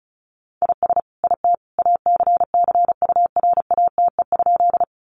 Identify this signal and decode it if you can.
Morse code, heard as a keyed tone, about 34 wpm, 700 Hz. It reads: IS IT ACCURATE?
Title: Sentences from Top 1200 Words 35wpm